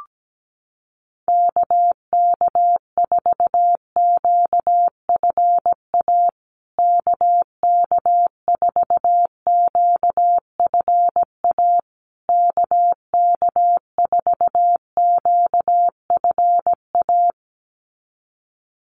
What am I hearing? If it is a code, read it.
Morse code, heard as a keyed tone, about 17 wpm, 700 Hz. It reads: KK4QFA KK4QFA KK4QFA
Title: USA Call Signs Repeated 3x 17wpm